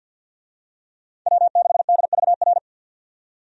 Morse code, heard as a keyed tone, about 49 wpm, 700 Hz. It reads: W6DVR
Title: USA Call Signs - 5 Characters - 50wpm